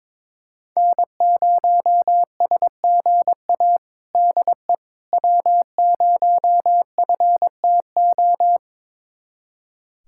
Morse code, heard as a keyed tone, about 22 wpm, 700 Hz. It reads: N0SGA DE W0FTO
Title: USA Callsign DE Callsign - 5 Characters - 22wpm